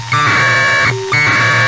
Packet radio (AX.25, as in APRS).
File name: Sohla-1 1200bd packet HiMode by ja0caw